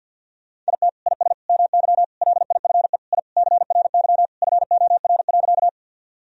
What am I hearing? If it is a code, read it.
Morse code, heard as a keyed tone, about 50 wpm, 700 Hz. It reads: IT IS MY LIFE I CRY FOR.
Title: Sentences from Top 1000 Words - Review Part 2 - 50wpm